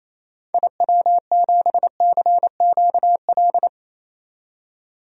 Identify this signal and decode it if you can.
Morse code, heard as a keyed tone, about 28 wpm, 700 Hz. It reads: IW7CQL
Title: DXCC-100 Call Signs - 6 Characters - 28wpm